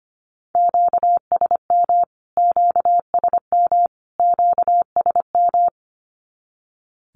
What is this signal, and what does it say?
Morse code, heard as a keyed tone, about 25 wpm, 700 Hz. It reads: QSM QSM QSM
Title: Q-Codes Repeated 3x 25wpm